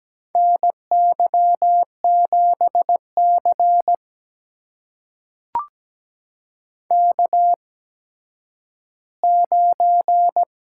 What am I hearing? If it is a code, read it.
Morse code, heard as a keyed tone, about 17 wpm, spaced as for 10 wpm, 700 Hz. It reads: NY7C K9
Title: USA Call Signs - 4 Characters - 8x Character Spacing - 17wpm